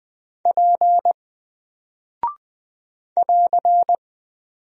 Morse code, heard as a keyed tone, about 20 wpm, 700 Hz. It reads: P <AR>
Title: Single Character (Letter:Number:Punctuation:Prosign) 20wpm